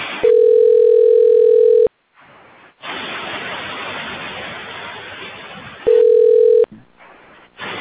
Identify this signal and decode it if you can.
Ringback tone (North America), 440 + 480 Hz.